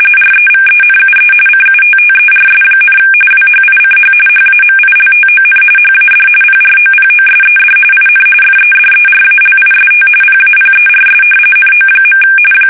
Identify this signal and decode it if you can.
RTTY, about 45.45 baud. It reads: REVENUE CUTTERS, WHICH HAMILTON HAD PROPOSED IN FEDERALIST PAPERS NO.
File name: fsk850 / usnavy1